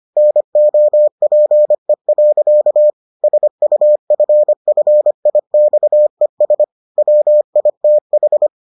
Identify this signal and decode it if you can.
Morse code, heard as a keyed tone, about 25 wpm, 600 Hz. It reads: NOPE. SUFFIXES WITH